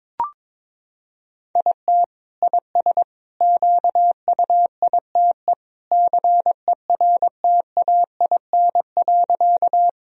Morse code, heard as a keyed tone, about 22 wpm, 700 Hz. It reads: IT IS QUITE CERTAIN.
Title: Sentences from Top 1000 Words - Review Part 2 - 22wpm